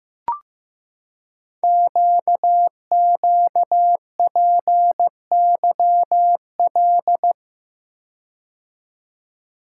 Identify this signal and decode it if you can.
Morse code, heard as a keyed tone, about 15 wpm, 700 Hz. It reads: QQPYL
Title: Code Groups - 5 Letters - 15wpm